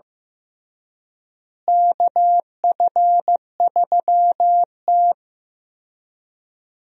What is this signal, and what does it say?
Morse code, heard as a keyed tone, about 15 wpm, 700 Hz. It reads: KF3T
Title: USA Call Signs - 4 Characters - Repeated 3x - 15wpm